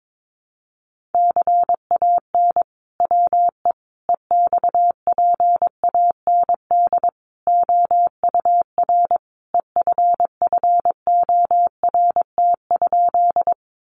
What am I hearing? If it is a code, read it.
Morse code, heard as a keyed tone, about 22 wpm, 700 Hz. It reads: CAN WE EXPAND OUR EFFORT?
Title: Sentences from Top 1400 Words 22wpm